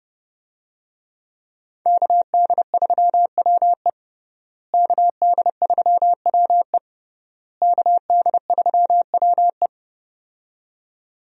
Morse code, heard as a keyed tone, about 28 wpm, 700 Hz. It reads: KD3WE KD3WE KD3WE
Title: USA Call Signs - 5 Characters - Repeated 3x - 30wpm